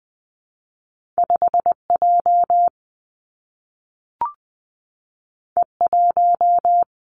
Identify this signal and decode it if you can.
Morse code, heard as a keyed tone, about 20 wpm, 700 Hz. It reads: 5 J E 1
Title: Two Challenging Letter-Number Combinations 20wpm